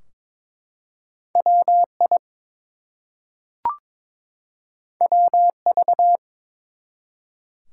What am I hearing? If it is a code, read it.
Morse code, heard as a keyed tone, about 22 wpm, 700 Hz. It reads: WI WV
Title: US States and Canadian Provinces 22wpm